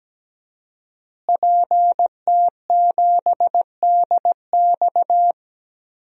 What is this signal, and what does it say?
Morse code, heard as a keyed tone, about 17 wpm, 700 Hz. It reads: PT7DX
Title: DXCC-100 Call Signs - 5 Characters - Repeated 3x - 17wpm